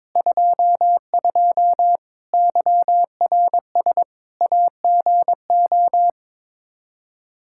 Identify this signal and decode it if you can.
Morse code, heard as a keyed tone, about 22 wpm, 700 Hz. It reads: 22 YRS AGO
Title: Practice QSO - Exchange 3 - More Info - 22wpm